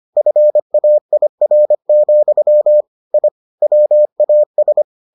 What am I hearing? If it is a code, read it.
Morse code, heard as a keyed tone, about 25 wpm, 600 Hz. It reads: FAIR, I WAS